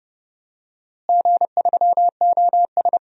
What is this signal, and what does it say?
Morse code, heard as a keyed tone, about 32 wpm, 700 Hz. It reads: G3OS
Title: DXCC-100 Call Signs - 4 Characters - Repeated 2x - 30wpm